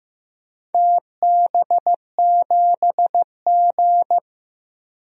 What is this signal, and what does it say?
Morse code, heard as a keyed tone, about 15 wpm, 700 Hz. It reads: TB7G
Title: DXCC-100 Call Signs - 4 Characters - Repeated 2x - 15wpm